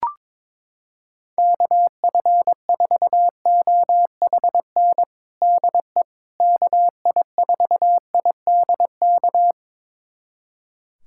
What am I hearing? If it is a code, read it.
Morse code, heard as a keyed tone, about 21 wpm, 700 Hz. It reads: KF4OHN DE KI4IDK
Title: USA Callsign DE Callsign - 6 Characters - 22wpm